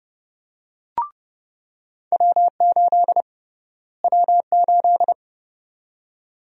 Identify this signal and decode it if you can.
Morse code, heard as a keyed tone, about 32 wpm, 700 Hz. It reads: W8 W8
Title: USA Call Sign Prefixes - Repeated 2x - 30wpm